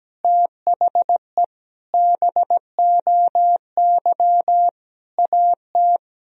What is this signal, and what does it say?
Morse code, heard as a keyed tone, about 17 wpm, 700 Hz. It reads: THE BOY AT
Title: Sentences from Top 300 Words and Abbreviations 17wpm